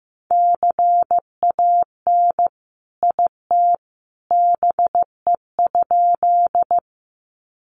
Morse code, heard as a keyed tone, about 15 wpm, 700 Hz. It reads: CAN IT BE?
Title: Sentences from Top 100 Words 15wpm